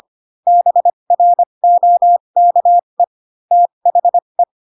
Morse code, heard as a keyed tone, about 25 wpm, spaced as for 22 wpm, 700 Hz. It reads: BROKE THE